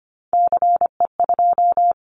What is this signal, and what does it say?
Morse code, heard as a keyed tone, about 25 wpm, 700 Hz. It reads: CE2
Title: Callsign Prefixes for Top 100 DXCC - 25wpm